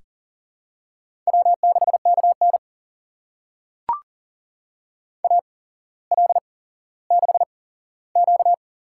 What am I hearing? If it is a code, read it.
Morse code, heard as a keyed tone, about 38 wpm, spaced as for 17 wpm, 700 Hz. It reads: W6KN AL6Q
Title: USA Call Signs - 4 Characters - 8x Character Spacing - 40wpm